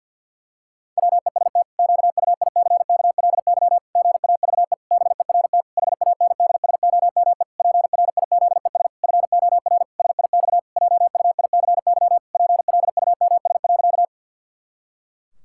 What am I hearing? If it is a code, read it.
Morse code, heard as a keyed tone, about 50 wpm, 700 Hz. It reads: WEST QUICKLY GAVE BERT HANDSOME PRIZES FOR SIX JUICY PLUMS.
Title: Pangram Challenge v2 - 50wpm